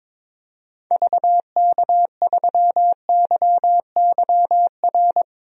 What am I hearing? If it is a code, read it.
Morse code, heard as a keyed tone, about 22 wpm, 700 Hz. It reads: VK3YYR
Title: DXCC-100 Call Signs - 6 Characters - Repeated 2x - 22wpm